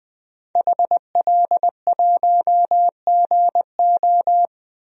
Morse code, heard as a keyed tone, about 20 wpm, 700 Hz. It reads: HL1GO
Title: DXCC-100 Call Signs - 5 Characters - 20wpm